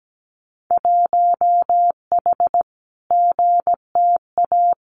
Morse code, heard as a keyed tone, about 17 wpm, 700 Hz. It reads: 1H GTA
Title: Winter Field Day Exchanges Encore 17wpm